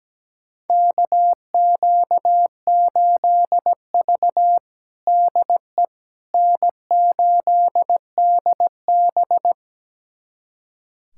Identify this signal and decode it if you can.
Morse code, heard as a keyed tone, about 17 wpm, 700 Hz. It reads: KQ8V DE N8DB
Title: USA Callsign DE Callsign - 4 Characters - 17wpm